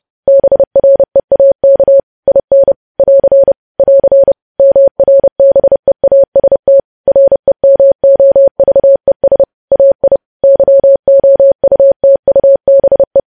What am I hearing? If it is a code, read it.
Morse code, heard as a keyed tone, about 30 wpm, 550 Hz. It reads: BREAK IN <AR> <AR> MRBEAST REMOVES AI YOUTUBE
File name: podcast / News headlines-1x-30x30